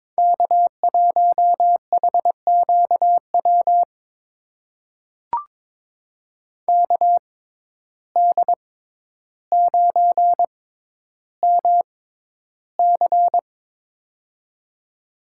Morse code, heard as a keyed tone, about 22 wpm, spaced as for 11 wpm, 700 Hz. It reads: K1HQW KD9MC
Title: USA Call Signs - 5 Characters - 6x Character Spacing - 22wpm